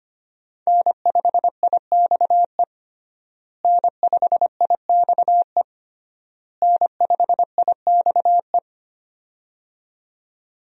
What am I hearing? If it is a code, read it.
Morse code, heard as a keyed tone, about 25 wpm, 700 Hz. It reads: N5IXE N5IXE N5IXE
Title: USA Call Signs - 5 Characters - Repeated 3x - 25wpm